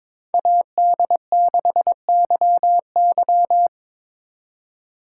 Morse code, heard as a keyed tone, about 25 wpm, 700 Hz. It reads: AD6YY
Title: Speed Racing - USA 2x2 Call Signs 22wpm